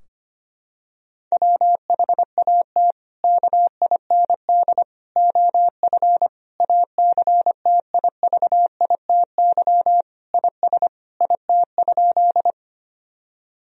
Morse code, heard as a keyed tone, about 25 wpm, 700 Hz. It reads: WHAT KIND OF ACTIVITY IS IT?
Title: Sentences from Top 1000 Words - Review Part 4 - 25wpm